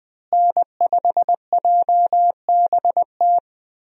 Morse code, heard as a keyed tone, about 20 wpm, 700 Hz. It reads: N5JBT
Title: USA Call Signs - 5 Characters - Repeated 3x - 20wpm